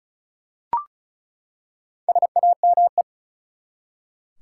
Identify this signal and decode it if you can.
Morse code, heard as a keyed tone, about 36 wpm, 700 Hz. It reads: SAME
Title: Four Letter Words - 35wpm